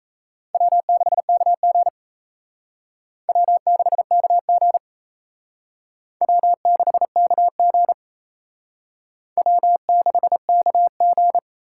Morse code, heard as a keyed tone, about 34 wpm, 700 Hz. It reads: W6KG W6KG W6KG W6KG
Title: Speed Racing - USA 1x2 Call Signs 28wpm